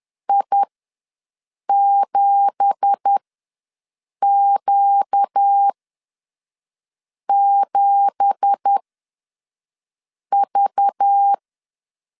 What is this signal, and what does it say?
Morse code, heard as a keyed tone, about 10 wpm, spaced as for 5 wpm, 800 Hz. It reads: I7Q7V
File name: LEZIONI / 9 2867 ws 10wpm 4215